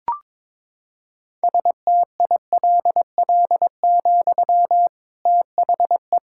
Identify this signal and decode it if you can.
Morse code, heard as a keyed tone, about 22 wpm, 700 Hz. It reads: STILL, THE
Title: Sentences from Top 1600 Words 22wpm